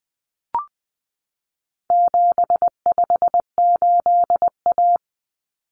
Morse code, heard as a keyed tone, about 20 wpm, 700 Hz. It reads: 758A
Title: Common Cut Numbers - 4-Digit 20wpm